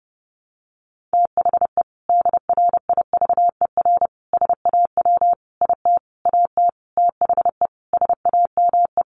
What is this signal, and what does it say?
Morse code, heard as a keyed tone, about 30 wpm, 700 Hz. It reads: THE DRIVER SAW IT AT THE SAME
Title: Sentences from Top 1000 Words 30wpm